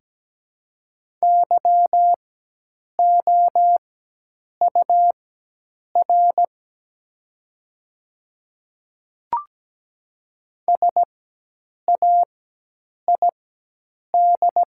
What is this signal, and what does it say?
Morse code, heard as a keyed tone, about 17 wpm, spaced as for 8 wpm, 700 Hz. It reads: YOUR SAID
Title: 4x Character Spacing - Top 100 Words - 17wpm